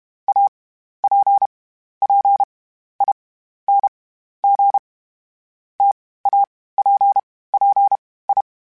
Morse code, heard as a keyed tone, about 32 wpm, spaced as for 15 wpm, 800 Hz. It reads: APPING TAPPI
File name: Difficult Combos / Difficult Combos-mp3s / T AOEI @30